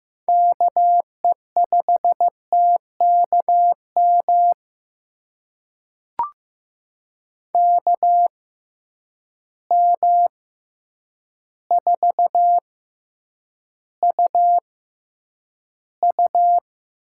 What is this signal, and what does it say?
Morse code, heard as a keyed tone, about 15 wpm, spaced as for 8 wpm, 700 Hz. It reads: KE5TKM KM4UU
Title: USA Call Signs - 6 Characters - 6x Character Spacing - 15wpm